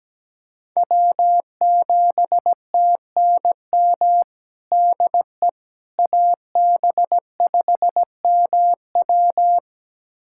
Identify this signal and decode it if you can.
Morse code, heard as a keyed tone, about 17 wpm, 700 Hz. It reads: W7TNM DE AB5MW
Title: USA Callsign DE Callsign - 5 Characters - 17wpm